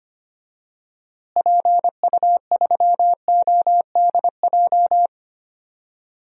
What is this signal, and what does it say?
Morse code, heard as a keyed tone, about 25 wpm, 700 Hz. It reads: PU3ODJ
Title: DXCC-100 Call Signs - 6 Characters - Repeated 3x - 25wpm